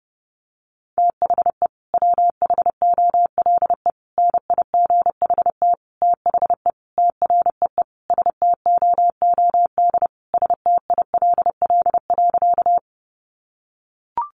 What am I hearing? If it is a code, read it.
Morse code, heard as a keyed tone, about 30 wpm, 700 Hz. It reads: THE WHOLE NIGHT THE TREE STOOD STILL.
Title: Sentences from Top 500 Words 30wpm